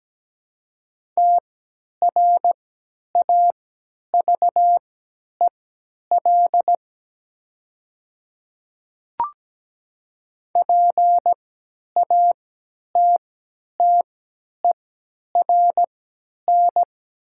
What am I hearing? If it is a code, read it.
Morse code, heard as a keyed tone, about 17 wpm, spaced as for 10 wpm, 700 Hz. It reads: TRAVEL PATTERN
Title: 3x Character Spacing - Top 500 Words - 17wpm